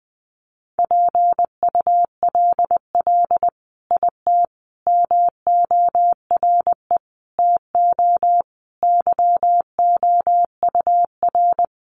Morse code, heard as a keyed tone, about 20 wpm, 700 Hz. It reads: PULL IT MORE TO YOUR
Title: Sentences from Top 400 Words 20wpm